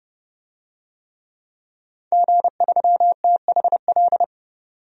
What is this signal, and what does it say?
Morse code, heard as a keyed tone, about 31 wpm, 700 Hz. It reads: G3THL
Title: DXCC-100 Call Signs - 5 Characters - 30wpm